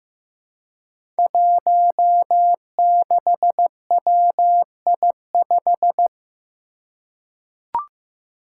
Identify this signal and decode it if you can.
Morse code, heard as a keyed tone, about 15 wpm, 700 Hz. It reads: 16WI5
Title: Five Challenging Letter-Number Combinations 15wpm